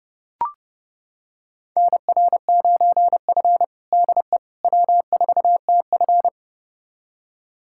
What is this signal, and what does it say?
Morse code, heard as a keyed tone, about 30 wpm, 700 Hz. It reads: NR9F DE W4TF
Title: USA Callsign DE Callsign - 4 Characters - 30wpm